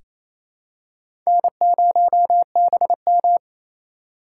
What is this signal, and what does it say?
Morse code, heard as a keyed tone, about 28 wpm, 700 Hz. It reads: N0BM
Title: USA Call Signs - 8x Character Spacing - 28wpm